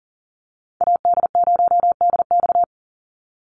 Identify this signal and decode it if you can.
Morse code, heard as a keyed tone, about 40 wpm, 700 Hz. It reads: AD0DX
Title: Call Signs Repeated 3x - 40wpm